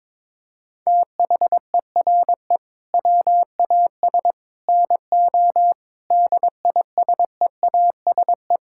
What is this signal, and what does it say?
Morse code, heard as a keyed tone, about 22 wpm, 700 Hz. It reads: THERE WAS NO DISEASE
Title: Sentences from Top 1000 Words - Review Part 6 - 22wpm